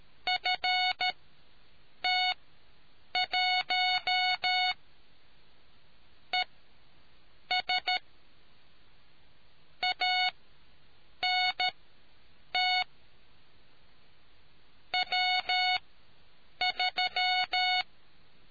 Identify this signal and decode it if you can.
Morse code, heard as a keyed tone, about 13 wpm, spaced as for 7 wpm, 2200 Hz. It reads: FT1 ES ANT W3